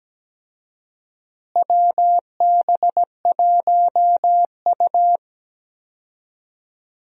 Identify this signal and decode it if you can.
Morse code, heard as a keyed tone, about 17 wpm, 700 Hz. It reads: WB1U
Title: USA Call Signs - 4 Characters - Repeated 3x - 17wpm